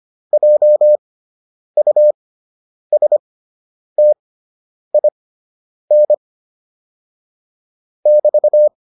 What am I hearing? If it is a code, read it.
Morse code, heard as a keyed tone, about 25 wpm, spaced as for 9 wpm, 600 Hz. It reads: JUSTIN <BT>